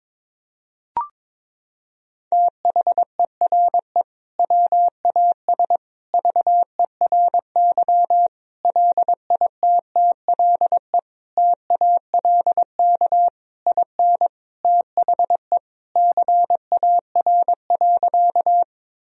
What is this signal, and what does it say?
Morse code, heard as a keyed tone, about 22 wpm, 700 Hz. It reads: THERE WAS VERY LITTLE TALK IN THE CAR.
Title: Sentences from Top 1000 Words - Review Part 1 - 22wpm